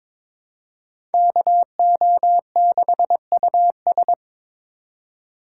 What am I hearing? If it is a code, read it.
Morse code, heard as a keyed tone, about 22 wpm, 700 Hz. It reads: KO6US
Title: USA Call Signs - 4x Character Spacing - 22wpm